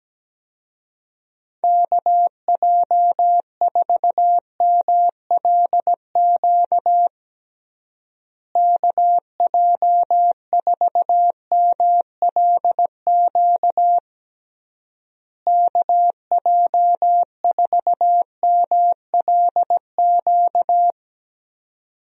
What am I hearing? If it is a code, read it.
Morse code, heard as a keyed tone, about 17 wpm, 700 Hz. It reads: KJ4MLQ KJ4MLQ KJ4MLQ
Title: USA Call Signs - 6 Characters - Repeated 3x - 17wpm